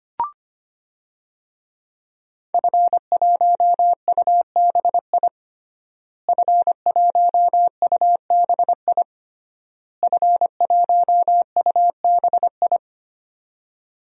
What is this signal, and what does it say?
Morse code, heard as a keyed tone, about 25 wpm, 700 Hz. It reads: F1UBI F1UBI F1UBI
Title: DXCC-100 Call Signs - 5 Characters - Repeated 3x - 25wpm